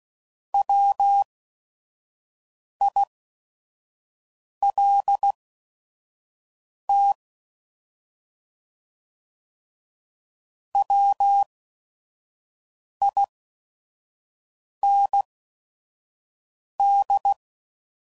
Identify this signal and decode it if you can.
Morse code, heard as a keyed tone, about 16 wpm, spaced as for 5 wpm, 800 Hz. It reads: WILT WIND